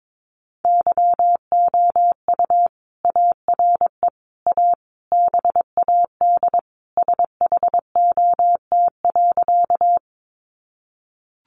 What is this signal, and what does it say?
Morse code, heard as a keyed tone, about 22 wpm, 700 Hz. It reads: YOU ARE A BAD SHOT.
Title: Sentences from Top 1100 Words 22wpm